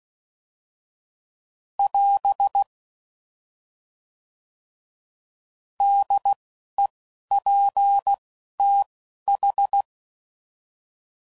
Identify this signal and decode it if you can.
Morse code, heard as a keyed tone, about 16 wpm, 800 Hz. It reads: <AS> D E P T H
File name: Prefixes / By First Letter / Ds